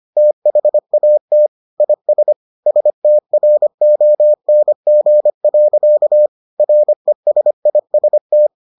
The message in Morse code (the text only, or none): THAT IS STRONG. RESIST